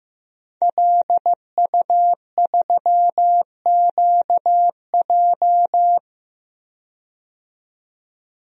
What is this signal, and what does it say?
Morse code, heard as a keyed tone, about 15 wpm, 700 Hz. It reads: LU3QJ
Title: DXCC-100 Call Signs - 5 Characters - 15wpm